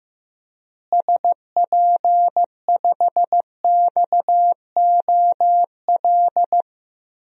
Morse code, heard as a keyed tone, about 15 wpm, 700 Hz. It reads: SP5XOL
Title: DXCC-100 Call Signs - 6 Characters - Repeated 3x - 15wpm